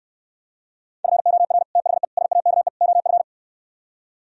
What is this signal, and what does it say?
Morse code, heard as a keyed tone, about 68 wpm, 700 Hz. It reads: FOR THE SAKE OF